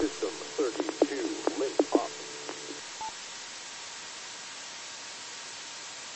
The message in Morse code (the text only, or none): none